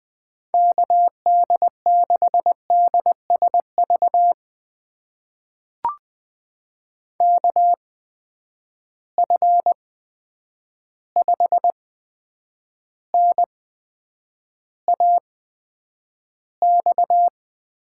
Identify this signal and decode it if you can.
Morse code, heard as a keyed tone, about 20 wpm, spaced as for 9 wpm, 700 Hz. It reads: KD6DSV KF5NAX
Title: USA Call Signs - 6 Characters - 8x Character Spacing - 20wpm